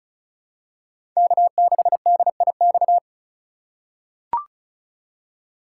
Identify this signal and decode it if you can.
Morse code, heard as a keyed tone, about 36 wpm, 700 Hz. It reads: K6DIX
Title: USA Call Signs - 3x Character Spacing - 35wpm